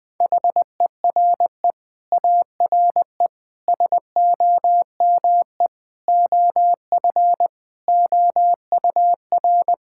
Morse code, heard as a keyed tone, about 20 wpm, 700 Hz. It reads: HERE ARE SOME OF OUR
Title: Sentences with 9-Word Phrases - 20wpm